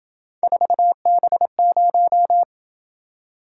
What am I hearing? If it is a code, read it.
Morse code, heard as a keyed tone, about 28 wpm, 700 Hz. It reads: 4B0